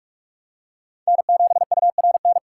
Morse code, heard as a keyed tone, about 42 wpm, 700 Hz. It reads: N7URN
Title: USA Call Signs - 5 Characters - 3x Character Spacing - 45wpm